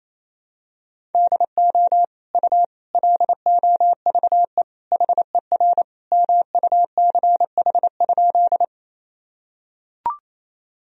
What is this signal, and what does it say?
Morse code, heard as a keyed tone, about 28 wpm, 700 Hz. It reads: DO U LOVE HER MUCH?
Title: Sentences from Top 400 Words and Abbreviations 28wpm